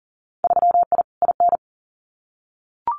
Morse code, heard as a keyed tone, about 38 wpm, 700 Hz. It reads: 3I IN
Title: Winter Field Day Exchanges 40wpm